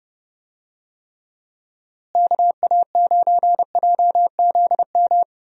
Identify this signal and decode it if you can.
Morse code, heard as a keyed tone, about 29 wpm, 700 Hz. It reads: KA9JZM
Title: USA Call Signs - 6 Characters - Repeated 2x - 30wpm